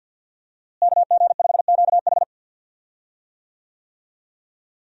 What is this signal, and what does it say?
Morse code, heard as a keyed tone, about 51 wpm, 700 Hz. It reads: KG5QH
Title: USA Call Signs - 5 Characters - Repeated 3x - 50wpm